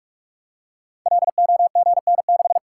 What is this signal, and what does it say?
Morse code, heard as a keyed tone, about 44 wpm, 700 Hz. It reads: LOGN6
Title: Code Groups - 5 Letter:Numbers - 45wpm